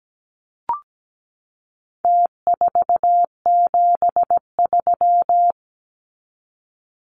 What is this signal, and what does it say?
Morse code, heard as a keyed tone, about 17 wpm, 700 Hz. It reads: T473
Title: Numbers - Contest Style - 4-Digit 17wpm